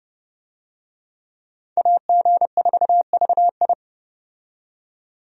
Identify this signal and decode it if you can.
Morse code, heard as a keyed tone, about 30 wpm, 700 Hz. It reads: AG4VI
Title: USA Call Signs - 5 Characters - 30wpm